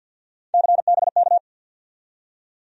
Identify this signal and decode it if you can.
Morse code, heard as a keyed tone, about 47 wpm, 700 Hz. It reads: CBK
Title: USA Call Sign Postfixes - 3 Letters 50wpm